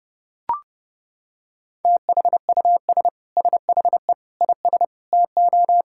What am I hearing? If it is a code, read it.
Morse code, heard as a keyed tone, about 30 wpm, 700 Hz. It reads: THUS SHE IS TO
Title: Sentences from Top 1000 Words - Review Part 5 - 30wpm